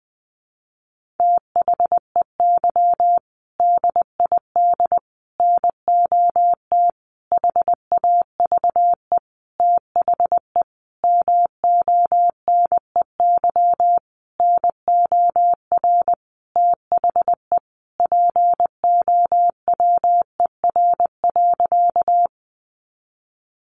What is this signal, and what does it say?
Morse code, heard as a keyed tone, about 20 wpm, 700 Hz. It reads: THEY DID NOT HAVE THE MONEY NOR THE POWER.
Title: Sentences from Top 1500 Words 20wpm